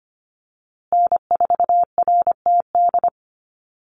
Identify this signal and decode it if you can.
Morse code, heard as a keyed tone, about 25 wpm, 700 Hz. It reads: N4RTD
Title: USA Call Signs Repeated 3x 25wpm